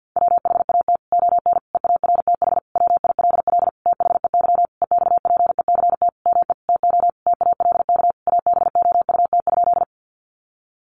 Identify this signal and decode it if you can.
Morse code, heard as a keyed tone, about 50 wpm, 700 Hz. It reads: WHAT ON EARTH WILL THEY EXPECT ME TO TALK ABOUT?
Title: Sentences from Top 1100 Words 50wpm